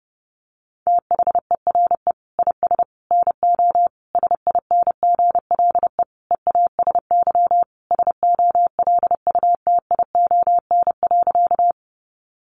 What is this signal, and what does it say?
Morse code, heard as a keyed tone, about 30 wpm, 700 Hz. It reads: THERE IS NO SINGLE EASY SOLUTION.
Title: Sentences from Top 1300 Words 30wpm